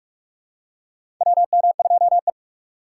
Morse code, heard as a keyed tone, about 46 wpm, 700 Hz. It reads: WM2E
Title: USA Call Signs - 2x Character Spacing - 45wpm